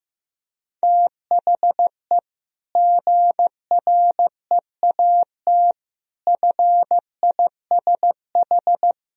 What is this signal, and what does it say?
Morse code, heard as a keyed tone, about 16 wpm, 700 Hz. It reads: THE GREAT FISH
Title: Sentences from Top 1000 Words - Review Part 1 - 15wpm